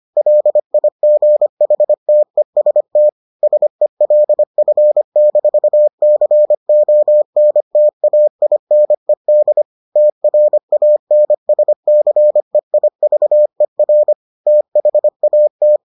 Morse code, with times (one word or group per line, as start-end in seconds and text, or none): 0.2-3.1 LIGHTEST
3.4-9.6 SELF-CONTAINED
10.0-14.1 TRANSCEIVER
14.5-15.8 THAT